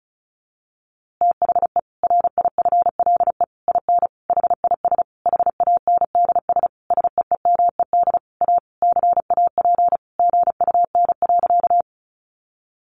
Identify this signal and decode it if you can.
Morse code, heard as a keyed tone, about 34 wpm, 700 Hz. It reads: THE RIFLE IN HIS HANDS SEEMED A CAP GUN.
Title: A Sound Of Thunder 35wpm